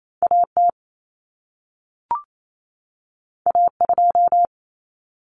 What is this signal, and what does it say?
Morse code, heard as a keyed tone, about 28 wpm, 700 Hz. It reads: AT A2
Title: Common Cut Numbers - 2-Digit 28wpm